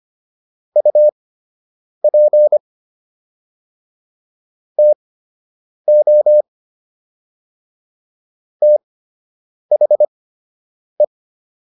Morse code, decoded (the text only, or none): UP TO THE